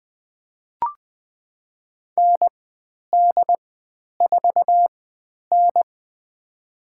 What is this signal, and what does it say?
Morse code, heard as a keyed tone, about 20 wpm, spaced as for 10 wpm, 700 Hz. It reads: ND4N
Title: 102 Lesson 17g - Letter u - Call Signs 20wpm